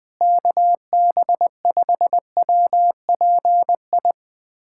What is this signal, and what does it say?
Morse code, heard as a keyed tone, about 20 wpm, 700 Hz. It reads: KB5WPI
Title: USA Callsign DE Callsign - 6 Characters - 20wpm